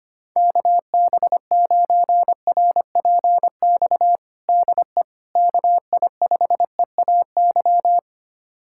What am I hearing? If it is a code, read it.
Morse code, heard as a keyed tone, about 25 wpm, 700 Hz. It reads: KB9RPX DE KI5EAY
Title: USA Callsign DE Callsign - 6 Characters - 25wpm